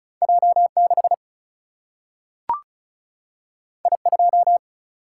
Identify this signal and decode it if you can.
Morse code, heard as a keyed tone, about 36 wpm, 700 Hz. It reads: J6 I2
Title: Single-Letter Single-Number 35wpm